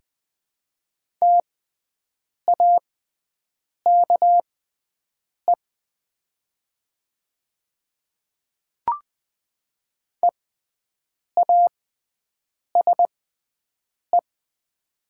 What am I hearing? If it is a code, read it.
Morse code, heard as a keyed tone, about 20 wpm, spaced as for 7 wpm, 700 Hz. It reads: TAKE EASE
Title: 6x Character Spacing - Top 500 Words - 20wpm